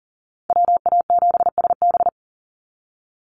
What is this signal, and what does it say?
Morse code, heard as a keyed tone, about 40 wpm, 700 Hz. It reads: WA7SB
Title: NA SOTA Chaser Callsigns 40wpm